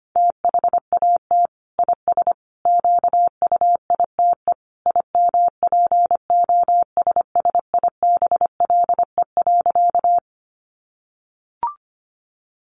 Morse code, heard as a keyed tone, about 25 wpm, 700 Hz. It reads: THAT IS QUITE IMPOSSIBLE.
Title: Sentences from Top 1100 Words 25wpm